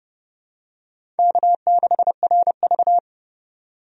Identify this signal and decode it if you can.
Morse code, heard as a keyed tone, about 30 wpm, 700 Hz. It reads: K6RV
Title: USA Call Signs - 4 Characters - 30wpm